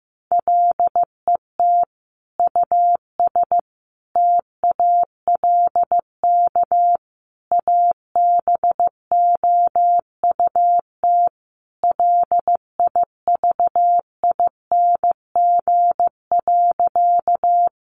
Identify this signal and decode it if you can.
Morse code, heard as a keyed tone, about 15 wpm, 700 Hz. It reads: LET US TALK ABOUT LIVING.
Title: Sentences from Top 1100 Words 15wpm